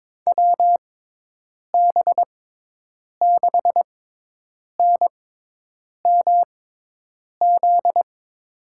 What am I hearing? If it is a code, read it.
Morse code, heard as a keyed tone, about 22 wpm, spaced as for 8 wpm, 700 Hz. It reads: WB6NMZ
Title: USA Call Signs - 6 Characters - 6x Character Spacing - 22wpm